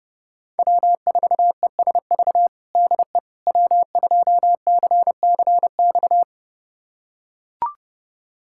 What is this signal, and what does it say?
Morse code, heard as a keyed tone, about 30 wpm, 700 Hz. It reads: W4ESV DE W2CCX
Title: USA Callsign DE Callsign - 5 Characters - 30wpm